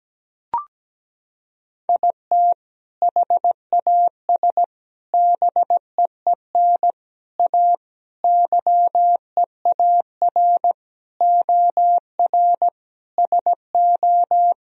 Morse code, heard as a keyed tone, about 17 wpm, 700 Hz. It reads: IT HAS BEEN A YEAR OR SO
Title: Sentences from Top 1000 Words - Review Part 1 - 17wpm